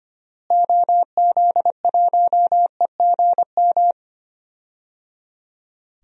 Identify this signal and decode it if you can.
Morse code, heard as a keyed tone, about 25 wpm, 700 Hz. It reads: OZ1EGM
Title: DXCC-100 Call Signs - 6 Characters - Repeated 3x - 25wpm